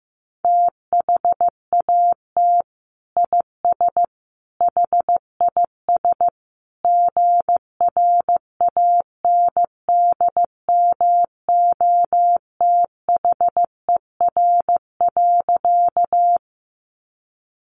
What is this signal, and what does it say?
Morse code, heard as a keyed tone, about 15 wpm, 700 Hz. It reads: THAT IS HIS GRANDMOTHER.
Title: Sentences from Top 1200 Words 15wpm